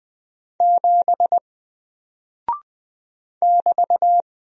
Morse code, heard as a keyed tone, about 20 wpm, 700 Hz. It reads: 7 <BT>
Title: Single Character (Letter:Number:Punctuation:Prosign) 20wpm